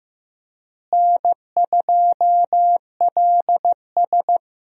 Morse code, heard as a keyed tone, about 15 wpm, 700 Hz. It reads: N2LS
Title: USA Call Signs - 4 Characters - Repeated 2x - 15wpm